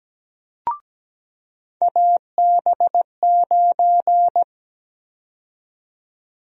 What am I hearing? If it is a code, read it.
Morse code, heard as a keyed tone, about 17 wpm, 700 Hz. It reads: AB9
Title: USA Call Sign Prefixes - 17wpm